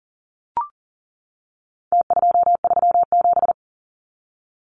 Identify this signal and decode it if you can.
Morse code, heard as a keyed tone, about 40 wpm, 700 Hz. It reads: T237
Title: Numbers - Contest Style - 4-Digit 40wpm